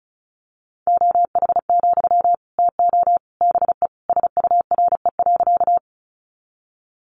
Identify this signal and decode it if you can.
Morse code, heard as a keyed tone, about 34 wpm, 700 Hz. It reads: OH, TO BE SURE.
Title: Sentences from Top 500 Words 35wpm